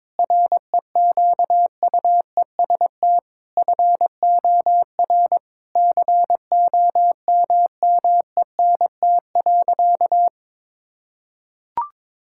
Morse code, heard as a keyed tone, about 22 wpm, 700 Hz. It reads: REQUEST FOR COMMENT.
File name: practice / 22wpm / Sentences with 8-Word Phrases - 22wpm